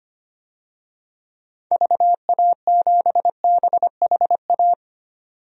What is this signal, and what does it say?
Morse code, heard as a keyed tone, about 25 wpm, 700 Hz. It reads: VA7BHA
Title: DXCC-100 Call Signs - 6 Characters - 25wpm